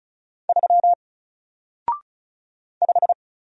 Morse code, heard as a keyed tone, about 35 wpm, 700 Hz. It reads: 3 5
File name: Single-Letter-Number-35wpm